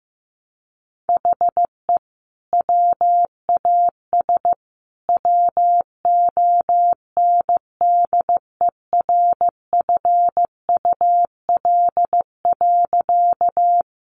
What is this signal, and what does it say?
Morse code, heard as a keyed tone, about 15 wpm, 700 Hz. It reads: HE WAS WONDERFUL.
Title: Sentences from Top 1100 Words 15wpm